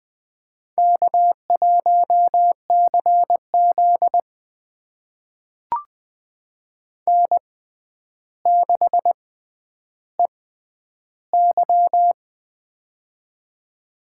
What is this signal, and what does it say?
Morse code, heard as a keyed tone, about 20 wpm, spaced as for 10 wpm, 700 Hz. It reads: K1CZ N6EY
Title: USA Call Signs - 4 Characters - 6x Character Spacing - 20wpm